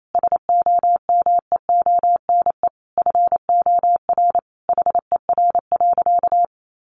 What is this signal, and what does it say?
Morse code, heard as a keyed tone, about 28 wpm, 700 Hz. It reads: SOMEONE FOR HER.
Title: Sentences from Top 600 Words 28wpm